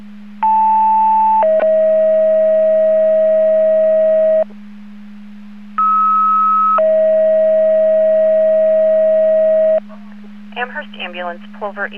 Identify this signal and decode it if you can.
Two-tone page, heard as a one-second tone, then a three-second tone, about 880 then 640 Hz, about 1290 then 640 Hz, with speech. It is heard from 0.4 to 9.8 s.